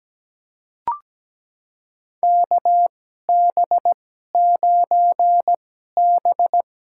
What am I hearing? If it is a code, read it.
Morse code, heard as a keyed tone, about 17 wpm, spaced as for 12 wpm, 700 Hz. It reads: KB9B
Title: USA Call Signs - 2x Character Spacing - 17wpm